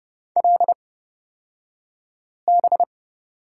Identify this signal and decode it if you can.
Morse code, heard as a keyed tone, about 30 wpm, spaced as for 5 wpm, 700 Hz. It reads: LB
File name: Lesson 37d - Letter x - Two Characters 30wpm